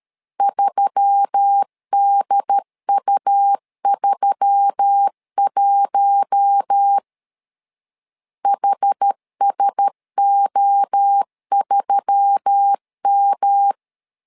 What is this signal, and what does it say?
Morse code, heard as a keyed tone, about 13 wpm, 800 Hz. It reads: 3DU31 HSO3M
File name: LEZIONI / 4 Q1S3 12wpm 2023